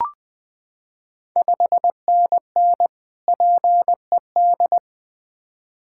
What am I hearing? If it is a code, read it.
Morse code, heard as a keyed tone, about 20 wpm, 690 Hz. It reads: 5NN PED
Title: Ontario QSO Party Exchange 20wpm